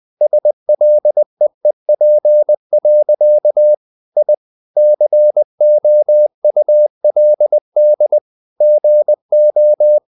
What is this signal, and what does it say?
Morse code, heard as a keyed tone, about 20 wpm, 600 Hz. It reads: SLEEP. I COULD GO
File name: the old man and the sea part 2 20wpm